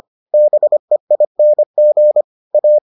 Morse code, heard as a keyed tone, about 25 wpm, 600 Hz. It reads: BEING A